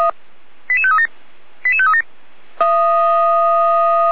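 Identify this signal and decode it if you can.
Five-tone selective call (ZVEI), heard as quick runs of five tones, tones of about 70 ms. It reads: 88537 88537
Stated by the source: ZVEI five-tone call received by SDR, from the Feuerwehr Orlamünde audio archive.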